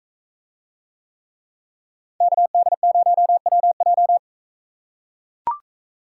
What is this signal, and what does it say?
Morse code, heard as a keyed tone, about 41 wpm, 700 Hz. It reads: KD0WJ